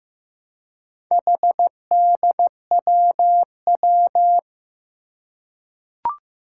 Morse code, heard as a keyed tone, about 15 wpm, 700 Hz. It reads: HDWW
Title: Four Challenging Letter-Number Combinations 15wpm